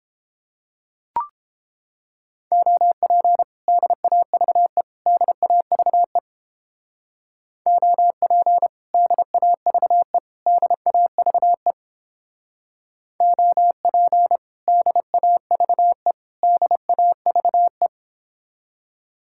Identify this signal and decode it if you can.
Morse code, heard as a keyed tone, about 29 wpm, 700 Hz. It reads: OP DAVE DAVE OP DAVE DAVE OP DAVE DAVE